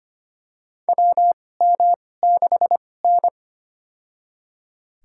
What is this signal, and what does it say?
Morse code, heard as a keyed tone, about 25 wpm, spaced as for 18 wpm, 700 Hz. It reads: WM6N
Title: USA Call Signs - 2x Character Spacing - 25wpm